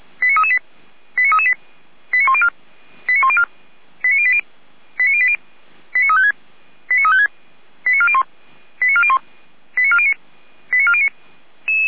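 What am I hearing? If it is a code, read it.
Five-tone selective call (ZVEI), heard as quick runs of five tones, tones of about 70 ms. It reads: 89338 89338 89104 89104 89080 89080 89356 89356 89401 89401 89409 89409
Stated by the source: ZVEI five-tone selective call received by SDR.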